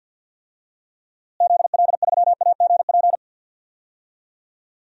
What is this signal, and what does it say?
Morse code, heard as a keyed tone, about 46 wpm, 700 Hz. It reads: ZL3AGP
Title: DXCC-100 Call Signs - 6 Characters - Repeated 2x - 50wpm